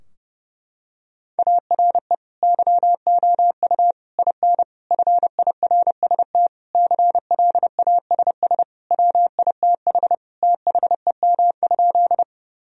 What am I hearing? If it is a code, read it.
Morse code, heard as a keyed tone, about 32 wpm, 700 Hz. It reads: ARE YOU IN FIRST CLASS WITH THEM?
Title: Sentences from Top 1000 Words - Review Part 2 - 30wpm